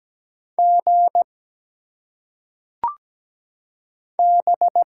Morse code, heard as a keyed tone, about 17 wpm, 700 Hz. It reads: G B
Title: Single Letters - Phonetic Pronunciation 17wpm